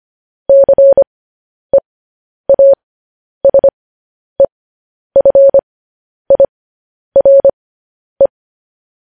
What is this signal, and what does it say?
Morse code, heard as a keyed tone, about 25 wpm, spaced as for 10 wpm, 550 Hz. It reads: CEASEFIRE
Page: news-headlines-in-morse-code-at-10-wpm